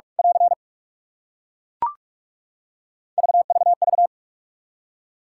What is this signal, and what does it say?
Morse code, heard as a keyed tone, about 41 wpm, 700 Hz. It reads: <AR> VVV
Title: Practice QSO - Exchange 2 - Info - 45wpm